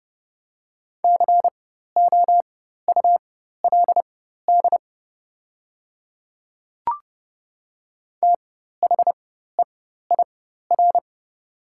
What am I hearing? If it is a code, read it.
Morse code, heard as a keyed tone, about 30 wpm, spaced as for 14 wpm, 700 Hz. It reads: COULD THEIR
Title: 4x Character Spacing - Top 100 Words - 30wpm